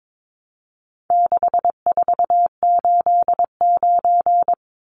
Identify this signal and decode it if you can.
Morse code, heard as a keyed tone, about 22 wpm, 700 Hz. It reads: 6489
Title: Numbers - 4-Digit 22wpm